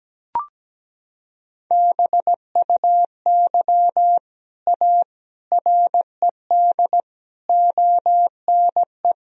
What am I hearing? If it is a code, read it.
Morse code, heard as a keyed tone, about 17 wpm, 700 Hz. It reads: BUY A RED ONE
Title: Sentences from Top 1000 Words - Review Part 2 - 17wpm